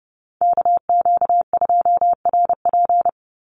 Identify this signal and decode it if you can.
Morse code, heard as a keyed tone, about 30 wpm, 700 Hz. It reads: KQ2RP
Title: NA SOTA Chaser Callsigns 30wpm